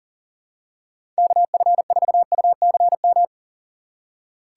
Morse code, heard as a keyed tone, about 40 wpm, 690 Hz. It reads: KF4UCM
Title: USA Call Signs - 6x Character Spacing - 40wpm